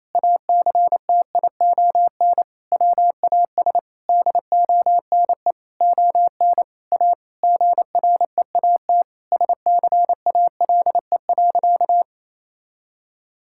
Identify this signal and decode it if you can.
Morse code, heard as a keyed tone, about 28 wpm, 700 Hz. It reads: ACTION WAS DONE ON A GREAT SCALE.
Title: Sentences from Top 1000 Words - Review Part 5 - 28wpm